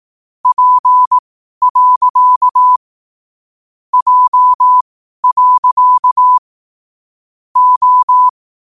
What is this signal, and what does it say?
Morse code, heard as a keyed tone, about 18 wpm, spaced as for 12 wpm, 1000 Hz. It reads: P. J. O